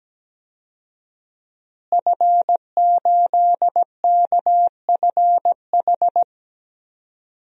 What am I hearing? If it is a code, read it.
Morse code, heard as a keyed tone, about 17 wpm, 700 Hz. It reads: F8KFH
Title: DXCC-100 Call Signs - 5 Characters - Repeated 2x - 17wpm